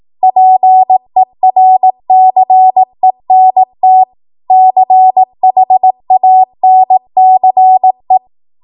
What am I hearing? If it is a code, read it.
Morse code, heard as a keyed tone, about 18 wpm, 750 Hz. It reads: PERCENT CHANCE
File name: Morse / 18WPM